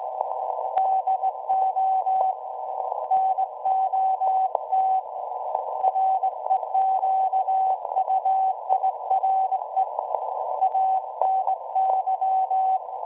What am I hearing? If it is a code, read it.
Morse code, heard as a keyed tone, about 18 wpm, 760 Hz. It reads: DO NOT REQUIRE ANY
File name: WebSDR records / SAQ at PI4THT part od 1005z 17-2kHz